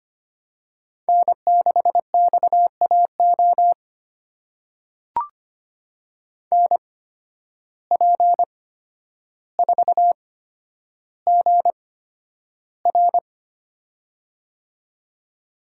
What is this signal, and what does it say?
Morse code, heard as a keyed tone, about 25 wpm, spaced as for 11 wpm, 700 Hz. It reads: N6XAO NP4GR
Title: USA Call Signs - 5 Characters - 8x Character Spacing - 25wpm